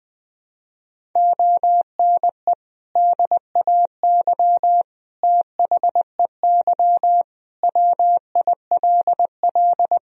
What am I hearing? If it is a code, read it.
Morse code, heard as a keyed tone, about 20 wpm, 700 Hz. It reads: ONE DAY THEY WILL
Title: Sentences from Top 1000 Words - Review Part 1 - 20wpm